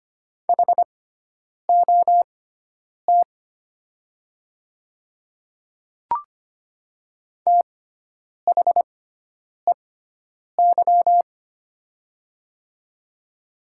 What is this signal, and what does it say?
Morse code, heard as a keyed tone, about 25 wpm, spaced as for 9 wpm, 700 Hz. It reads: HOT THEY
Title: 6x Character Spacing - Top 100 Words - 25wpm